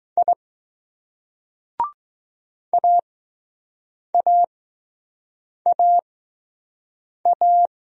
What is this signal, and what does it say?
Morse code, heard as a keyed tone, about 20 wpm, spaced as for 6 wpm, 700 Hz. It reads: I AAAA